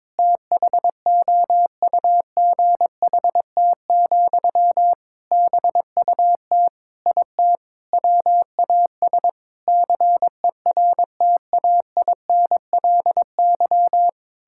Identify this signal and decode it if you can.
Morse code, heard as a keyed tone, about 22 wpm, 700 Hz. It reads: THOUGHT, BUT IT WAS CERTAINLY
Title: Sentences from Top 1000 Words - Review Part 6 - 22wpm